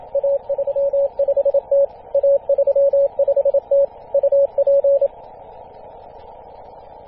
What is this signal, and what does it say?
Morse code, heard as a keyed tone, about 27 wpm, 560 Hz. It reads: A35T A35T UP
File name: posts / 2016-feb-27-0448z-a35t-24-898mhz-cw